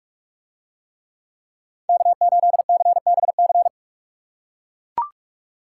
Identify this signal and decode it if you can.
Morse code, heard as a keyed tone, about 40 wpm, 700 Hz. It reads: K8CBC